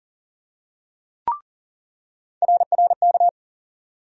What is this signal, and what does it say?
Morse code, heard as a keyed tone, about 40 wpm, 700 Hz. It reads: RRK
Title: USA Call Sign Postfixes - 3 Letters 40wpm